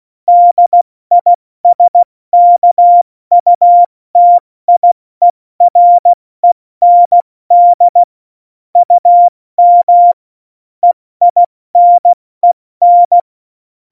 Morse code, heard as a keyed tone, about 16 wpm, spaced as for 14 wpm, 700 Hz. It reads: DISKUTIEREND UM EINEN
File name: Morsekurs / Witze / Witz 13. Oktober 2020, 70 F80